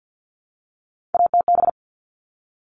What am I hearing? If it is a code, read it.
Morse code, heard as a keyed tone, about 50 wpm, 700 Hz. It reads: ATB
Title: All Cut Numbers - 3-Digit 50wpm